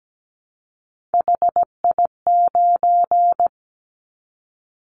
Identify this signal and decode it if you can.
Morse code, heard as a keyed tone, about 17 wpm, 700 Hz. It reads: HI9
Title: Callsign Prefixes for Top 100 DXCC - 17wpm